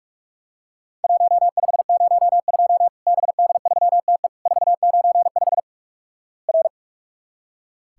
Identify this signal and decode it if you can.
Morse code, heard as a keyed tone, about 46 wpm, 700 Hz. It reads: 1502 BD3TE 495 R
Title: Worked All Europe DX Contest - QTC Exchange - 45wpm